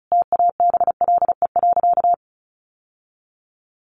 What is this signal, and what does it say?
Morse code, heard as a keyed tone, about 34 wpm, 700 Hz. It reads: TABLE.
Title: Sentences from Top 1500 Words 35wpm